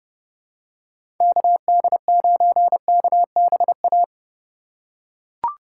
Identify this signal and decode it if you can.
Morse code, heard as a keyed tone, about 30 wpm, 700 Hz. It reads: KD9KBA
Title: USA Call Signs - 6 Characters - 2x Character Spacing - 30wpm